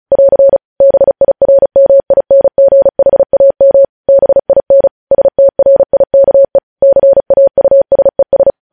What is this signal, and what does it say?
Morse code, heard as a keyed tone, about 36 wpm, 550 Hz. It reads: <AR> BIRMINGHAM BIN STRIKE CAUSES
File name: podcast / News headlines-1x-35x35